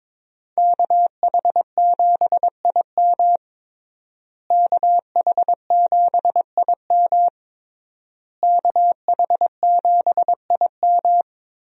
Morse code, heard as a keyed tone, about 22 wpm, 700 Hz. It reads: KH7IM KH7IM KH7IM
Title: USA Call Signs - 5 Characters - Repeated 3x - 22wpm